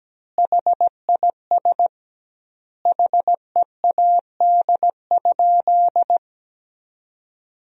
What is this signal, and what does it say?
Morse code, heard as a keyed tone, about 17 wpm, 700 Hz. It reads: HIS HEAD?
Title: 2x Word-Spacing - Sentences from Top 300 Words - 17wpm